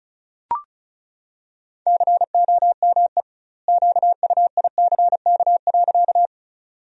Morse code, heard as a keyed tone, about 34 wpm, 700 Hz. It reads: COME QUICK.
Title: 2x Word-Spacing - Sentences from Top 500 Words - 35wpm